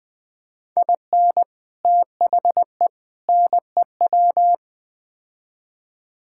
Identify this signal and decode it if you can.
Morse code, heard as a keyed tone, about 21 wpm, 700 Hz. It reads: IN THE NEW
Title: Build A Phrase - 5-Word Phrases - 20wpm